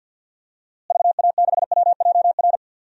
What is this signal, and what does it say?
Morse code, heard as a keyed tone, about 50 wpm, 700 Hz. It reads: VA6WJR
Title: DXCC-100 Call Signs - 6 Characters - Repeated 2x - 50wpm